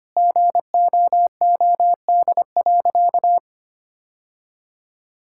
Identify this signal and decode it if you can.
Morse code, heard as a keyed tone, about 25 wpm, 700 Hz. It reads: GOOD.
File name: practice / 25wpm / Sentences with 3-Word Phrases - 25wpm